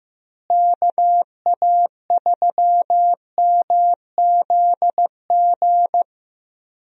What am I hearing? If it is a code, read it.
Morse code, heard as a keyed tone, about 15 wpm, 700 Hz. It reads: KA3MZG
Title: USA Call Signs - 6 Characters - Repeated 3x - 15wpm